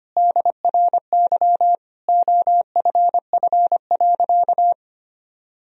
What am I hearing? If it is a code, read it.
Morse code, heard as a keyed tone, about 25 wpm, 700 Hz. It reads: DRY OFF.
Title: Sentences from Top 1000 Words - Review Part 3 - 25wpm